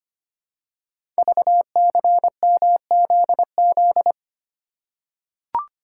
Morse code, heard as a keyed tone, about 25 wpm, 700 Hz. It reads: VCMZZ
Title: Code Groups - 5 Letters - 25wpm